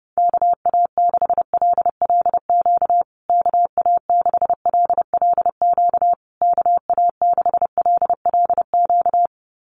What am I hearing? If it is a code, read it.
Morse code, heard as a keyed tone, about 30 wpm, 700 Hz. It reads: KA6LLQ KA6LLQ KA6LLQ
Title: USA Call Signs Repeated 3x 30wpm